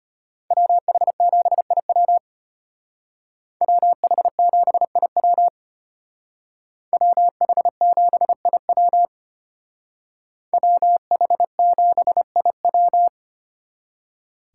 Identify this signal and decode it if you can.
Morse code, heard as a keyed tone, about 33 wpm, 700 Hz. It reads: WH7IW WH7IW WH7IW WH7IW